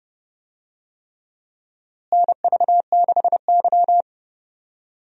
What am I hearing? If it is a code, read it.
Morse code, heard as a keyed tone, about 29 wpm, 700 Hz. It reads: NV6Y
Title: USA Call Signs - 4 Characters - Repeated 3x - 30wpm